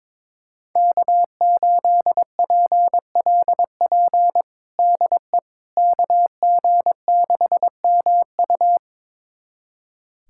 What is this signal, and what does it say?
Morse code, heard as a keyed tone, about 22 wpm, 700 Hz. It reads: K8PLP DE KG6MU
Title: USA Callsign DE Callsign - 5 Characters - 22wpm